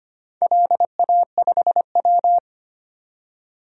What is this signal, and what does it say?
Morse code, heard as a keyed tone, about 25 wpm, 700 Hz. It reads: LA5W
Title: DXCC-100 Call Signs - 4 Characters - 25wpm